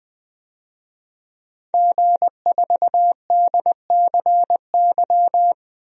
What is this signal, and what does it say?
Morse code, heard as a keyed tone, about 20 wpm, 700 Hz. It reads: G4DCY
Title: DXCC-100 Call Signs - 5 Characters - 20wpm